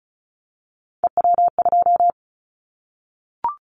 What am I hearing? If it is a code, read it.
Morse code, heard as a keyed tone, about 34 wpm, 700 Hz. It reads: EW2
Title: Callsign Prefixes for Top 100 DXCC - 35wpm